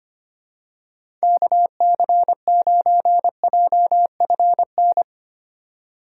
Morse code, heard as a keyed tone, about 25 wpm, 700 Hz. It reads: KC9JFN
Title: USA Call Signs - 2x Character Spacing - 25wpm